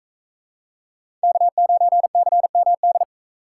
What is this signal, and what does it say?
Morse code, heard as a keyed tone, about 41 wpm, 700 Hz. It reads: K9CMD